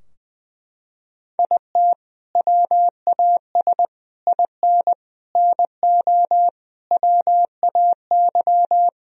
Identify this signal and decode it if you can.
Morse code, heard as a keyed tone, about 21 wpm, 700 Hz. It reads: IT WAS IN NO WAY
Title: Sentences from Top 1000 Words - Review Part 4 - 20wpm